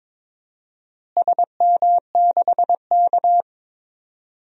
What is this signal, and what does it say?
Morse code, heard as a keyed tone, about 22 wpm, 700 Hz. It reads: SM6K
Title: DXCC-100 Call Signs - 4 Characters - Repeated 3x - 22wpm